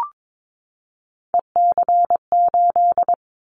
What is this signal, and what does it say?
Morse code, heard as a keyed tone, about 22 wpm, 700 Hz. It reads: EC8
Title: Callsign Prefixes for Top 100 DXCC - 22wpm